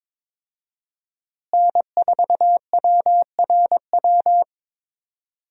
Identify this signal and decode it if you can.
Morse code, heard as a keyed tone, about 22 wpm, 700 Hz. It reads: N4WRW
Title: USA Call Signs - 5 Characters - Repeated 3x - 22wpm